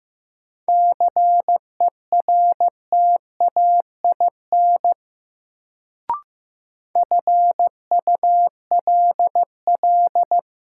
Morse code, heard as a keyed tone, about 15 wpm, 700 Hz. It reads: CERTAIN FULL
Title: Sentences from Top 500 Words - New Word Drill - 15wpm